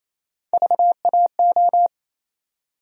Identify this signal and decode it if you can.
Morse code, heard as a keyed tone, about 28 wpm, 700 Hz. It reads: VAO
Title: USA Call Sign Postfixes - 3 Letters 28wpm